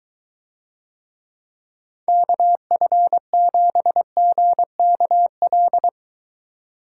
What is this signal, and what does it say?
Morse code, heard as a keyed tone, about 23 wpm, 700 Hz. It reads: KF7GKL